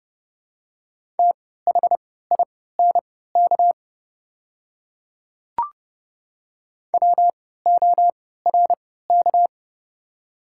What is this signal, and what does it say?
Morse code, heard as a keyed tone, about 30 wpm, spaced as for 17 wpm, 700 Hz. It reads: THINK WORK
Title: 3x Character Spacing - Top 200 Words - 30wpm